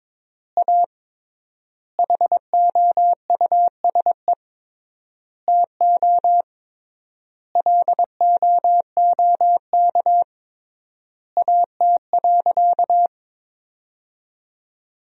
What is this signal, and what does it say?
Morse code, heard as a keyed tone, about 22 wpm, 700 Hz. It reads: A HOUSE TO LOOK AT.
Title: 3x Word-Spacing - Sentences from Top 200 Words - 22wpm